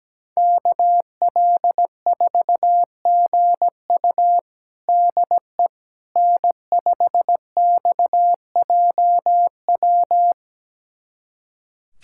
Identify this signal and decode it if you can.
Morse code, heard as a keyed tone, about 17 wpm, 700 Hz. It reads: KL4GU DE N5XJW
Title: USA Callsign DE Callsign - 5 Characters - 17wpm